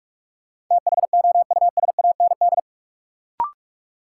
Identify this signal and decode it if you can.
Morse code, heard as a keyed tone, about 41 wpm, 700 Hz. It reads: THOUSAND
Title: Sentences from Top 600 Words - New Word Drill - 45wpm